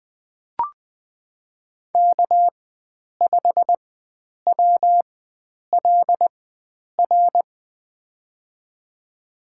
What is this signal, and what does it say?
Morse code, heard as a keyed tone, about 20 wpm, spaced as for 9 wpm, 700 Hz. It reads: K5WLR
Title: USA Call Signs - 5 Characters - 4x Character Spacing - 20wpm